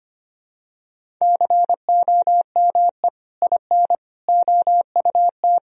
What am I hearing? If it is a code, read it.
Morse code, heard as a keyed tone, about 25 wpm, 700 Hz. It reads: COME IN OUT
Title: Sentences from Top 1000 Words - Review Part 1 - 25wpm